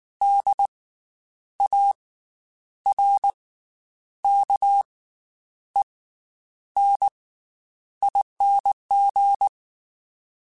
Morse code, heard as a keyed tone, about 19 wpm, 800 Hz. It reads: D A R K E N ING
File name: Special INGs / Special INGs-mp3s / Slow ing @18